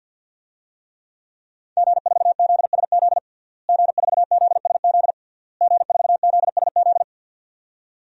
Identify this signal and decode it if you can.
Morse code, heard as a keyed tone, about 50 wpm, 700 Hz. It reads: G4ZSZ G4ZSZ G4ZSZ
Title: DXCC-100 Call Signs - 5 Characters - Repeated 3x - 50wpm